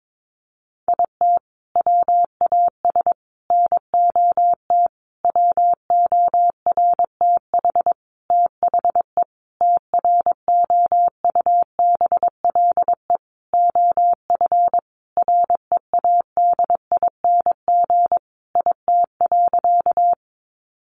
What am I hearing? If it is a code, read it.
Morse code, heard as a keyed tone, about 22 wpm, 700 Hz. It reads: IT WAS NOT WORTH THE TROUBLE OF READING IT.
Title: Sentences from Top 1200 Words 22wpm